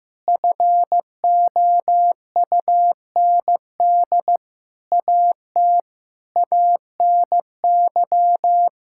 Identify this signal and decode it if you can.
Morse code, heard as a keyed tone, about 16 wpm, 700 Hz. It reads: FOUND AT ANY
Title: Sentences from Top 1000 Words - Review Part 5 - 15wpm